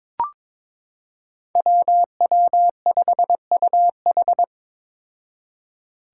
Morse code, heard as a keyed tone, about 22 wpm, 700 Hz. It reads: WW5UH
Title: Five Challenging Letter-Number Combinations 22wpm